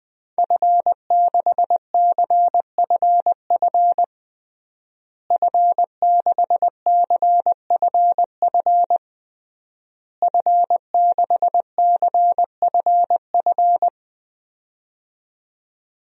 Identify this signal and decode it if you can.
Morse code, heard as a keyed tone, about 21 wpm, 700 Hz. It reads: F6CFF F6CFF F6CFF
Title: DXCC-100 Call Signs - 5 Characters - Repeated 3x - 20wpm